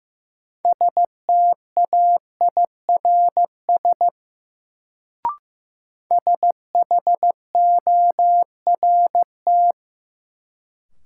Morse code, heard as a keyed tone, about 15 wpm, 700 Hz. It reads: STAIRS SHORT
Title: Sets of 5 Words Encore - New Word Drill - 15wpm